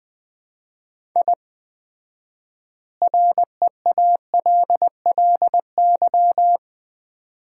Morse code, heard as a keyed tone, about 21 wpm, 700 Hz. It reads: I REALLY
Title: 4x Word-Spacing - Sets of 4 Words - 20wpm